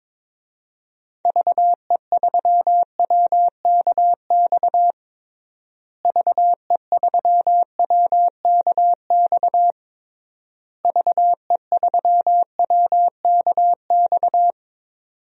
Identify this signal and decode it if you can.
Morse code, heard as a keyed tone, about 22 wpm, 700 Hz. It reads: VE3WKX VE3WKX VE3WKX
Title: DXCC-100 Call Signs - 6 Characters - Repeated 3x - 22wpm